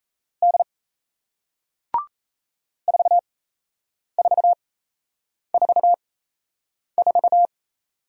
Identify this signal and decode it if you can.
Morse code, heard as a keyed tone, about 38 wpm, spaced as for 8 wpm, 700 Hz. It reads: D 4444